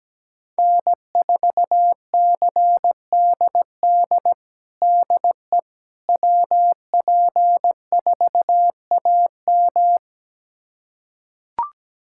Morse code, heard as a keyed tone, about 17 wpm, 700 Hz. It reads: N4CDD DE WP4AM
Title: USA Callsign DE Callsign - 5 Characters - 17wpm